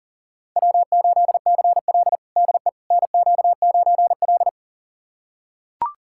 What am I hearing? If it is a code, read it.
Morse code, heard as a keyed tone, about 37 wpm, 700 Hz. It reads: W8CL DE NQ9L
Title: USA Callsign DE Callsign - 4 Characters - 40wpm